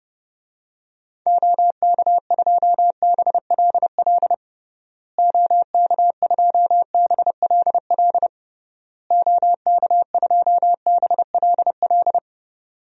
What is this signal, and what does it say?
Morse code, heard as a keyed tone, about 30 wpm, 700 Hz. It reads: OK2BLL OK2BLL OK2BLL
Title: DXCC-100 Call Signs - 6 Characters - Repeated 3x - 30wpm